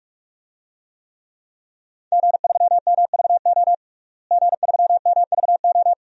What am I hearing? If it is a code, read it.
Morse code, heard as a keyed tone, about 42 wpm, 700 Hz. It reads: G3MVO G3MVO
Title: DXCC-100 Call Signs - 5 Characters - Repeated 2x - 45wpm